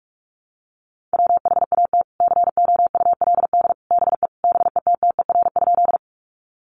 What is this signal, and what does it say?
Morse code, heard as a keyed tone, about 44 wpm, 700 Hz. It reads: WHAT COULD BE BETTER?
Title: Sentences from Top 400 Words 45wpm